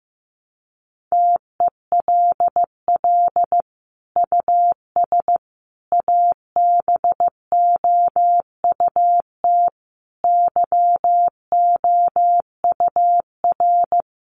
Morse code, heard as a keyed tone, about 15 wpm, 700 Hz. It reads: TELL US ABOUT YOUR
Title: Sentences from Top 1500 Words 15wpm